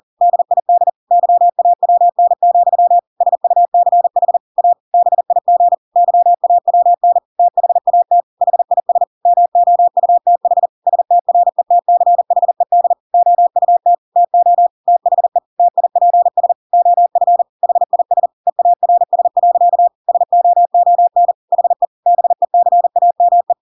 Morse code, heard as a keyed tone, about 40 wpm, 700 Hz. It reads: DID YAWN, SUCH A BIG YAWN THAT HIS MOUTH STRETCHED OUT TO THE TIPS OF HIS EARS. SOON HE BECAME
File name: Pinocchio / Pinocchio ch5 40 40 ik1hge